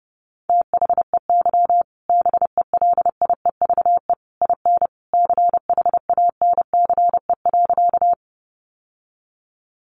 Morse code, heard as a keyed tone, about 28 wpm, 700 Hz. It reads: THEY BELIEVE IN CHANCE.
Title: Sentences from Top 1000 Words 30wpm